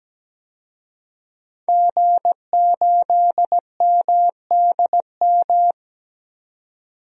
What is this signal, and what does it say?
Morse code, heard as a keyed tone, about 17 wpm, 700 Hz. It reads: G8MDM
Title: DXCC-100 Call Signs - 5 Characters - 17wpm